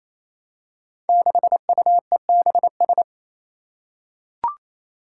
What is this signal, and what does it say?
Morse code, heard as a keyed tone, about 28 wpm, 700 Hz. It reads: 6UEBS
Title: Five Challenging Letter-Number Combinations 28wpm